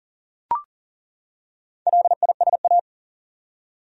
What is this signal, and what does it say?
Morse code, heard as a keyed tone, about 40 wpm, 700 Hz. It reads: LISA
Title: USA Common Names - Max 4 Letters - 40wpm